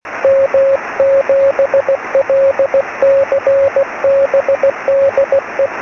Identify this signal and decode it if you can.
Morse code, heard as a keyed tone, about 16 wpm, 550 Hz, with radio noise. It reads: M7LCB DE
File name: sm7lcb / g3xdy 23cm